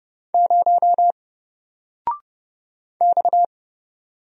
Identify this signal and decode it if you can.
Morse code, heard as a keyed tone, about 30 wpm, 700 Hz. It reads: 0 X
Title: Single Character (Letter:Number:Punctuation:Prosign) 30wpm